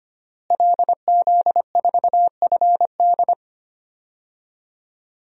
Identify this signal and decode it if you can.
Morse code, heard as a keyed tone, about 25 wpm, 700 Hz. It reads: LZ4FD
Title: DXCC-100 Call Signs - 5 Characters - Repeated 3x - 25wpm